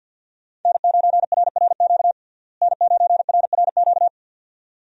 Morse code, heard as a keyed tone, about 52 wpm, 700 Hz. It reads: N9RRQ N9RRQ
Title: USA Call Signs - 5 Characters - Repeated 2x - 50wpm